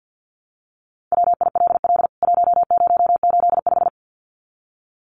Morse code, heard as a keyed tone, about 49 wpm, 700 Hz. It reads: WILL 1085
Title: CWT Exchange 50wpm